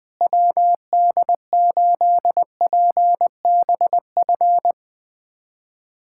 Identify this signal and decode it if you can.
Morse code, heard as a keyed tone, about 20 wpm, 700 Hz. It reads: WD8PBF
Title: USA Callsign DE Callsign - 6 Characters - 20wpm